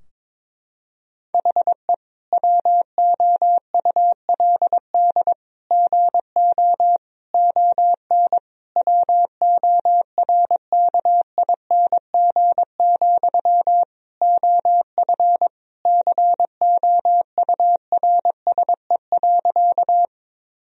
Morse code, heard as a keyed tone, about 22 wpm, 700 Hz. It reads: HE WOULD GO ON WORKING, OF COURSE.
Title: Sentences from Top 1000 Words - Review Part 4 - 22wpm